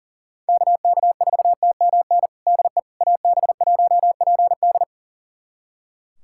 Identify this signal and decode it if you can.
Morse code, heard as a keyed tone, about 40 wpm, 700 Hz. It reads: KK4TMN DE AB1PD
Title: USA Callsign DE Callsign 40wpm